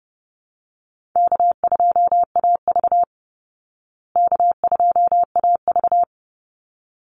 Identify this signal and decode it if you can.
Morse code, heard as a keyed tone, about 30 wpm, 700 Hz. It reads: K2AV K2AV
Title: Easy Call Signs Repeated 2x - 30wpm